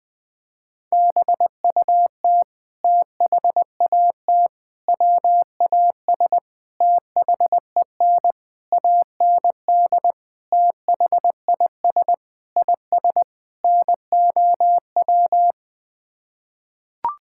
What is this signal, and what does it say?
Morse code, heard as a keyed tone, about 21 wpm, 700 Hz. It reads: BUT THAT WAS THEN AND THIS IS NOW
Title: Sets of 8 Words - 20wpm